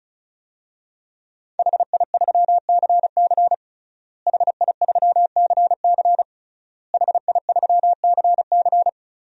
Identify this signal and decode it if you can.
Morse code, heard as a keyed tone, about 34 wpm, 700 Hz. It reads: HI3CC HI3CC HI3CC
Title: DXCC-100 Call Signs - 5 Characters - Repeated 3x - 35wpm